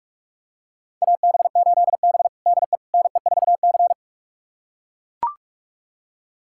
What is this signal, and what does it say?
Morse code, heard as a keyed tone, about 41 wpm, 700 Hz. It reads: AB8B DE NE4C
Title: USA Callsign DE Callsign - 4 Characters - 45wpm